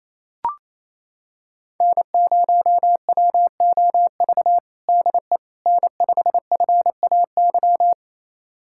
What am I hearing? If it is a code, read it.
Morse code, heard as a keyed tone, about 28 wpm, 700 Hz. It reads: N0WOV DE N5FAY
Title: USA Callsign DE Callsign - 5 Characters - 28wpm